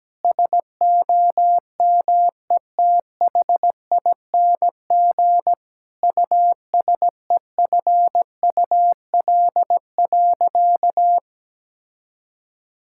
Morse code, heard as a keyed tone, about 17 wpm, 700 Hz. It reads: SOMETHING USEFUL.
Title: Sentences from Top 1000 Words - Review Part 5 - 17wpm